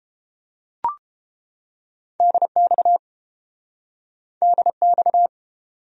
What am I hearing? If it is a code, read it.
Morse code, heard as a keyed tone, about 32 wpm, 700 Hz. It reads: DX DX